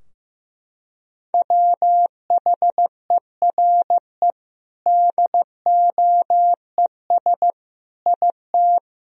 Morse code, heard as a keyed tone, about 15 wpm, 700 Hz. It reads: WHERE DOES IT
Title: Sentences from Top 1000 Words - Review Part 2 - 15wpm